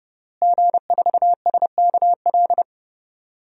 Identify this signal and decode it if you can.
Morse code, heard as a keyed tone, about 30 wpm, 700 Hz. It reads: G4SKL
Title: DXCC-100 Call Signs - 5 Characters - Repeated 3x - 30wpm